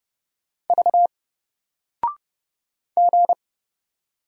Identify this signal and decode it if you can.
Morse code, heard as a keyed tone, about 30 wpm, 700 Hz. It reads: V G
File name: Single-Letter-Number-30wpm